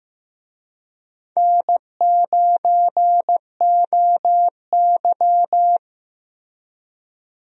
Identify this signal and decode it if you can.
Morse code, heard as a keyed tone, about 15 wpm, 700 Hz. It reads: N9OY
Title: USA Call Signs - 4 Characters - Repeated 2x - 15wpm